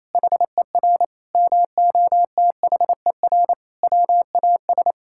Morse code, heard as a keyed tone, about 28 wpm, 700 Hz. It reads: HER MOTHER WAS
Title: Sentences from Top 1000 Words - Review Part 5 - 28wpm